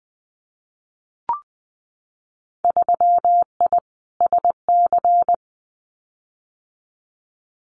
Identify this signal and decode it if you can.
Morse code, heard as a keyed tone, about 20 wpm, 700 Hz. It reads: 3I SC
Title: Winter Field Day Exchanges 20wpm